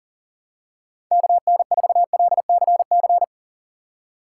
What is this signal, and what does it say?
Morse code, heard as a keyed tone, about 40 wpm, 700 Hz. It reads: KN4LCC
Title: USA Call Signs - 6 Characters - 6x Character Spacing - 40wpm